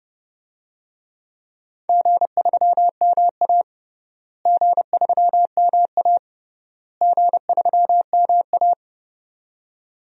Morse code, heard as a keyed tone, about 30 wpm, 700 Hz. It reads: G3MA G3MA G3MA
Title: DXCC-100 Call Signs - 4 Characters - Repeated 3x - 30wpm